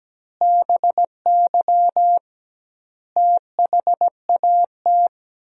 Morse code, heard as a keyed tone, about 17 wpm, 700 Hz. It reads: BY THAT
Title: 2x Word-Spacing - Sentences from Top 400 Words - 17wpm